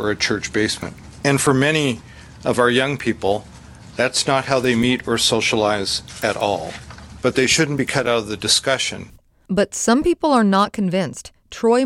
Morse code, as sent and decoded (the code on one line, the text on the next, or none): none